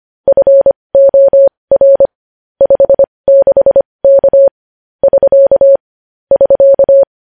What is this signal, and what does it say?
Morse code, heard as a keyed tone, about 25 wpm, spaced as for 20 wpm, 550 Hz. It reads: FOR 56K <SK> <SK>
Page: news-headlines-in-morse-code-at-20-wpm